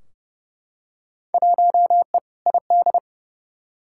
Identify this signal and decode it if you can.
Morse code, heard as a keyed tone, about 30 wpm, 700 Hz. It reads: 1E ID
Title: Field Day Exchanges - 30wpm